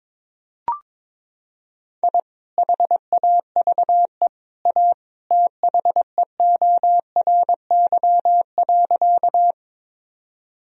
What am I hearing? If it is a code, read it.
Morse code, heard as a keyed tone, about 22 wpm, 700 Hz. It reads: I HAVE A THEORY.
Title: Sentences from Top 1000 Words - Review Part 3 - 22wpm